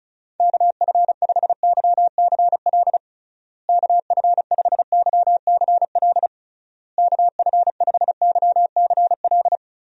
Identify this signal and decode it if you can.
Morse code, heard as a keyed tone, about 36 wpm, 700 Hz. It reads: KF5YCL KF5YCL KF5YCL
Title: USA Call Signs - 6 Characters - Repeated 3x - 35wpm